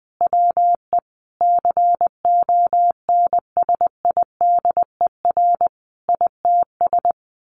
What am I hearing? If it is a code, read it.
Morse code, heard as a keyed tone, about 19 wpm, 700 Hz. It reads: WE CONSIDER ITS
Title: Sentences from Top 1100 Words 20wpm